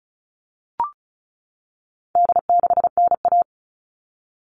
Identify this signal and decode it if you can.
Morse code, heard as a keyed tone, about 34 wpm, 700 Hz. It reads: D6NA
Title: All Cut Numbers - 4-Digit 35wpm